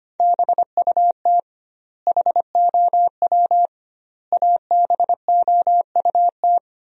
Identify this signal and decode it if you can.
Morse code, heard as a keyed tone, about 25 wpm, 700 Hz. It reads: BUT HOW ABOUT
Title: 2x Word-Spacing - Sentences from Top 100 Words - 25wpm